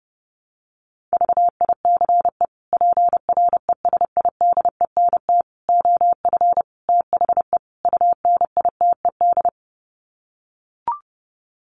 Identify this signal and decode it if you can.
Morse code, heard as a keyed tone, about 30 wpm, 700 Hz. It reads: VICE PRESIDENT OF THE UNITED
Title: Sets of 5 Words Encore2 30wpm